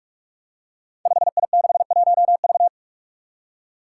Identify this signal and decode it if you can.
Morse code, heard as a keyed tone, about 44 wpm, 700 Hz. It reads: 5I61V
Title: Five Challenging Letter-Number Combinations 45wpm